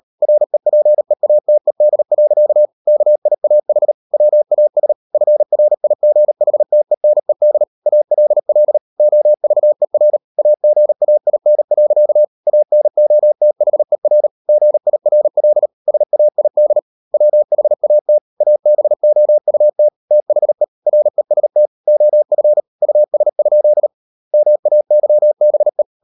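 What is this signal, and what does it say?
Morse code, heard as a keyed tone, about 38 wpm, 600 Hz. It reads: REPEATED. KIAH WAS FRIGHTENED ALL OVER AGAIN. ANOTHER GIRL SAID WHAT ABOUT THE REST OF US? MAYBE
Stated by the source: Morse code practice at 38 WPM, sending Ken Follett's novel Never in English, from CW med Gnister.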